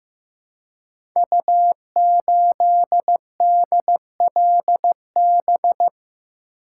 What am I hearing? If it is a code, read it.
Morse code, heard as a keyed tone, about 15 wpm, 700 Hz. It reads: U8DLB
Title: Code Groups - 5 Letter:Numbers - 15wpm